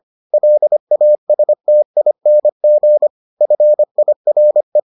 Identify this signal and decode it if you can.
Morse code, heard as a keyed tone, about 25 wpm, 600 Hz. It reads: LASTING FIRE